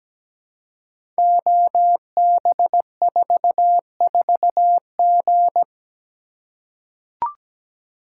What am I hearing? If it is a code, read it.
Morse code, heard as a keyed tone, about 17 wpm, 700 Hz. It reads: OB44G
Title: Code Groups - 5 Letter:Numbers - 17wpm